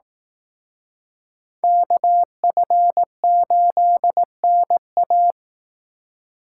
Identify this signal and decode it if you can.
Morse code, heard as a keyed tone, about 18 wpm, 700 Hz. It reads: KF8NA
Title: Speed Racing - USA 2x2 Call Signs 15wpm